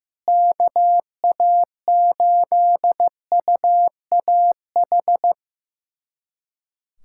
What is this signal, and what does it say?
Morse code, heard as a keyed tone, about 15 wpm, 700 Hz. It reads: KA8UAH
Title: USA Callsign DE Callsign 15wpm